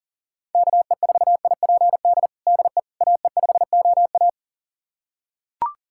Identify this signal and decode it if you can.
Morse code, heard as a keyed tone, about 37 wpm, 700 Hz. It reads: KE4IPD DE AE5OA
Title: USA Callsign DE Callsign 40wpm